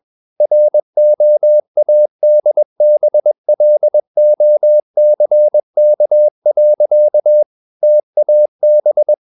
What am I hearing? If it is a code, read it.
Morse code, heard as a keyed tone, about 21 wpm, 600 Hz. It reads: ROADBLOCK. TAB
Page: Romanen Never af Ken Follet på engelsk.